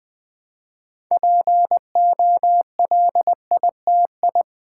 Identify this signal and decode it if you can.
Morse code, heard as a keyed tone, about 20 wpm, 700 Hz. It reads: POLITI
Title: Build A Word - 3 Letters At A Time - 8-Letter Words - 20wpm